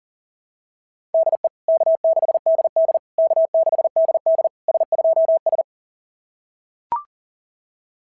Morse code, heard as a keyed tone, about 40 wpm, 690 Hz. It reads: DE K6DD K6DD S2S
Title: SOTA QSO Practice 40wpm